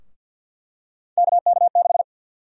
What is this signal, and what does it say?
Morse code, heard as a keyed tone, about 51 wpm, 700 Hz. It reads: KK6
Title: USA Call Sign Prefixes - Repeated 2x - 50wpm